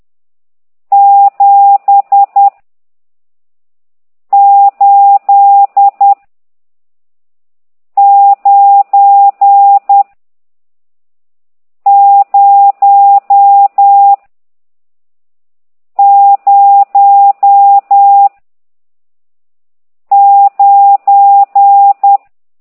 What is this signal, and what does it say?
Morse code, heard as a keyed tone, about 10 wpm, 800 Hz. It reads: 7 8 9 0 0 9
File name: ct2idw / exerc14